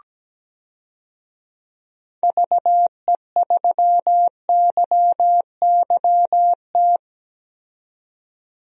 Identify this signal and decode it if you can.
Morse code, heard as a keyed tone, about 17 wpm, 700 Hz. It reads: VE3YYT
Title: DXCC-100 Call Signs - 6 Characters - 17wpm